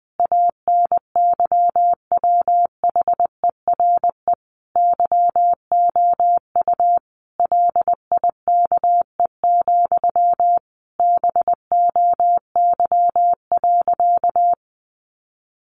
Morse code, heard as a keyed tone, about 20 wpm, 700 Hz. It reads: ANYWHERE YOU LIKE, BOY.
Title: Sentences from Top 1400 Words 20wpm